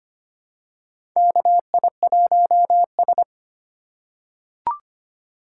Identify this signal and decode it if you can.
Morse code, heard as a keyed tone, about 25 wpm, 700 Hz. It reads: KI1S
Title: USA Call Signs - 4 Characters - 2x Character Spacing - 25wpm